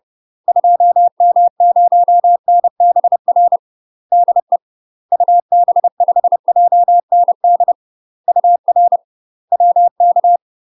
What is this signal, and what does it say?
Morse code, heard as a keyed tone, about 30 wpm, 700 Hz. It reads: 2M0NBR DE UB5JND UR WK
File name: UK MASTER QSOs TRAINING / 30WPM / UK MASTER QSOs 0035